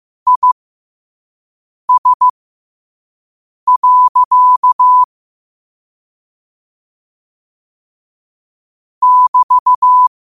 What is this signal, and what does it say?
Morse code, heard as a keyed tone, about 15 wpm, spaced as for 5 wpm, 1000 Hz. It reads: IS. <BT>
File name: qotdfiles / 5wpm